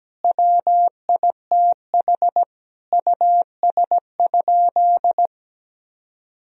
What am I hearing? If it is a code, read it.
Morse code, heard as a keyed tone, about 17 wpm, 700 Hz. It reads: WITH US?
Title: Sentences from Top 1000 Words - Review Part 3 - 17wpm